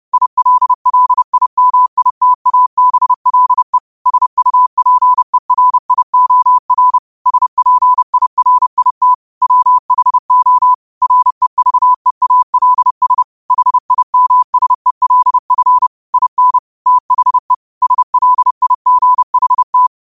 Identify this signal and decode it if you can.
Morse code, heard as a keyed tone, about 30 wpm, 1000 Hz. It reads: ILLIMITABLE SUPERIOR SPIRIT WHO REVEALS HIMSELF IN THE SLIGHT